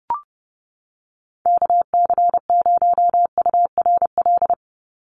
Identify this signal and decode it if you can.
Morse code, heard as a keyed tone, about 30 wpm, 700 Hz. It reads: KC0URL
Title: Call Signs - 30wpm